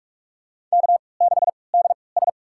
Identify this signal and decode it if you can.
Morse code, heard as a keyed tone, about 44 wpm, spaced as for 25 wpm, 700 Hz. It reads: K6DS
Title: USA Call Signs - 4 Characters - 3x Character Spacing - 45wpm